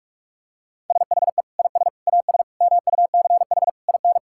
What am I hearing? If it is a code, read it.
Morse code, heard as a keyed tone, about 46 wpm, 700 Hz. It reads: SHE IS AS MUCH IN
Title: Sentences from Top 1000 Words - Review Part 5 - 45wpm